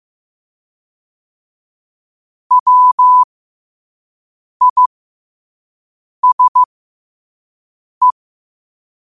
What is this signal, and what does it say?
Morse code, heard as a keyed tone, about 15 wpm, spaced as for 5 wpm, 1000 Hz. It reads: WISE
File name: 5wpm